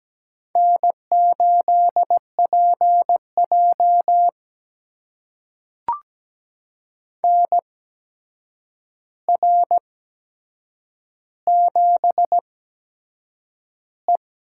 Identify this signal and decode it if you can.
Morse code, heard as a keyed tone, about 17 wpm, spaced as for 7 wpm, 700 Hz. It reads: N8PJ NR7E
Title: USA Call Signs - 4 Characters - 8x Character Spacing - 17wpm